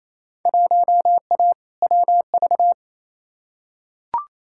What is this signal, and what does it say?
Morse code, heard as a keyed tone, about 28 wpm, 700 Hz. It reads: 1A WV
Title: Field Day Exchanges Encore - 28wpm